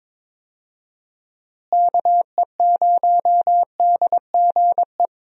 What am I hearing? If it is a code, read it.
Morse code, heard as a keyed tone, about 22 wpm, 700 Hz. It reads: KE0DGE
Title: USA Call Signs - 6 Characters - Repeated 2x - 22wpm